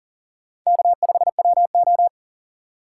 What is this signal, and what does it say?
Morse code, heard as a keyed tone, about 40 wpm, 700 Hz. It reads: K5WO
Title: USA Call Signs - 4x Character Spacing - 40wpm